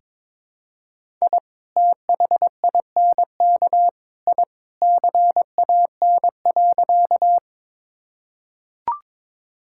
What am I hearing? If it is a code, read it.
Morse code, heard as a keyed tone, about 21 wpm, 700 Hz. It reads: I THINK I CAN.
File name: practice / 22wpm / Sentences with 3-Word Phrases - 22wpm